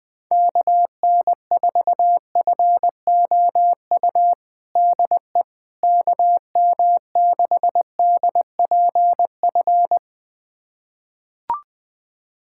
Morse code, heard as a keyed tone, about 20 wpm, 700 Hz. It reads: KN4FOU DE KM6DPF
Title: USA Callsign DE Callsign - 6 Characters - 20wpm